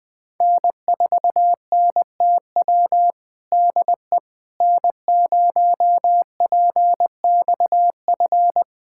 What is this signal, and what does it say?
Morse code, heard as a keyed tone, about 20 wpm, 700 Hz. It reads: N4NTW DE N0PXF
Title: USA Callsign DE Callsign - 5 Characters - 20wpm